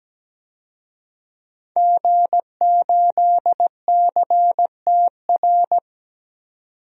Morse code, heard as a keyed tone, about 17 wpm, 700 Hz. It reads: G8CTR
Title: DXCC-100 Call Signs - 5 Characters - 17wpm